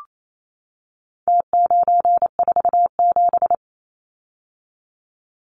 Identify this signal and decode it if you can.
Morse code, heard as a keyed tone, about 28 wpm, 700 Hz. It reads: T947
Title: Numbers - Contest Style - 4-Digit 28wpm